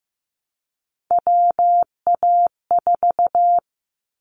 Morse code, heard as a keyed tone, about 15 wpm, 700 Hz. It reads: WA4
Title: Callsign Prefixes for Top 25 DXCC - Repeated 3x - 15wpm